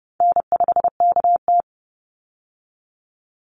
Morse code, heard as a keyed tone, about 30 wpm, 700 Hz. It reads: N5KT
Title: Easy Call Signs - 30wpm